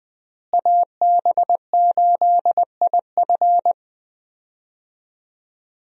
Morse code, heard as a keyed tone, about 20 wpm, 700 Hz. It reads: AB8IF
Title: USA Call Signs - 5 Characters - Repeated 2x - 20wpm